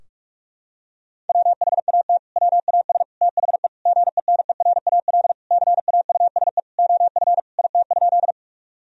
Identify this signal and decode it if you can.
Morse code, heard as a keyed tone, about 46 wpm, 700 Hz. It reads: WHAT WAS THE GENERAL CAUSE OF IT?
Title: Sentences from Top 1000 Words - Review Part 3 - 45wpm